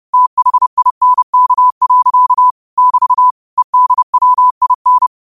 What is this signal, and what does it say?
Morse code, heard as a keyed tone, about 30 wpm, 1000 Hz. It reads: THINK. <BT> EDWIN
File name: qotdfiles / 30wpm